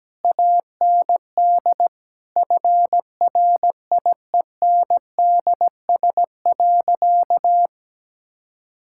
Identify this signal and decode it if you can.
Morse code, heard as a keyed tone, about 17 wpm, 700 Hz. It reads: AND FRIENDS.
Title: Sentences from Top 1000 Words - Review Part 5 - 17wpm